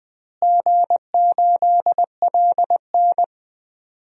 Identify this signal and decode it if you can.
Morse code, heard as a keyed tone, about 20 wpm, 700 Hz. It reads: G8LN
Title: DXCC-100 Call Signs - 4 Characters - Repeated 2x - 20wpm